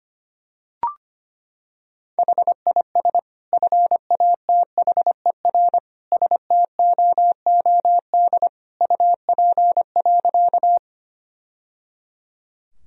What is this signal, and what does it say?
Morse code, heard as a keyed tone, about 25 wpm, 700 Hz. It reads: HIS FATHER STOOD UP.
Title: Sentences from Top 1000 Words - Review Part 2 - 25wpm